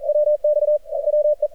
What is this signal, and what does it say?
Morse code, heard as a keyed tone, about 39 wpm, 600 Hz. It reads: WX WE
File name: archived / CRSnet